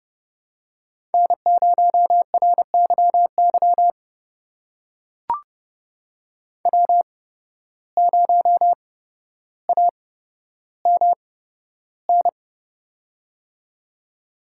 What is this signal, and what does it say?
Morse code, heard as a keyed tone, about 30 wpm, spaced as for 13 wpm, 700 Hz. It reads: N0RYY W0AMN
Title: USA Call Signs - 5 Characters - 8x Character Spacing - 30wpm